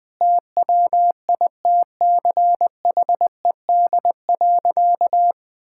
Morse code, heard as a keyed tone, about 20 wpm, 700 Hz. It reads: TWITCHED.
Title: Sentences with 2-Word Phrases - 20wpm